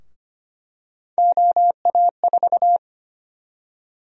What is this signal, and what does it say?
Morse code, heard as a keyed tone, about 25 wpm, 700 Hz. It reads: OA4
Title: USA Call Sign Prefixes - Repeated 2x - 25wpm